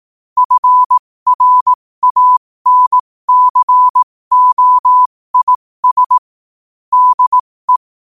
Morse code, heard as a keyed tone, about 18 wpm, spaced as for 15 wpm, 1000 Hz. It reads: FRANCOIS DE
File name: qotdfiles / 15wpm